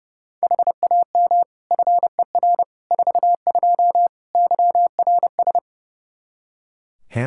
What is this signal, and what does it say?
Morse code, heard as a keyed tone, about 32 wpm, 700 Hz. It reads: HAM FER 42 YRS
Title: Practice QSO - Exchange 3 - More Info - 30wpm